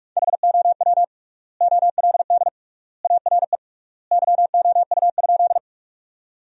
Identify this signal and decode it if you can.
Morse code, heard as a keyed tone, about 44 wpm, 700 Hz. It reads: HOW OLD ARE YOU?
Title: 3x Word-Spacing - Sentences from Top 200 Words - 45wpm